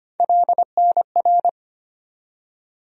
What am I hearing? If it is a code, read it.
Morse code, heard as a keyed tone, about 25 wpm, 700 Hz. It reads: LNR
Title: USA Call Sign Postfixes - 3 Letters 25wpm